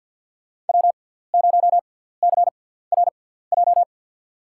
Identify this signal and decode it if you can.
Morse code, heard as a keyed tone, about 52 wpm, spaced as for 17 wpm, 700 Hz. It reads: W0CRJ
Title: USA Call Signs - 6x Character Spacing - 50wpm